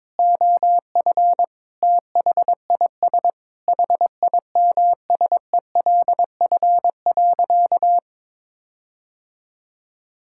Morse code, heard as a keyed tone, about 22 wpm, 700 Hz. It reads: OF THIS HIMSELF.
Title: Sentences from Top 1600 Words 22wpm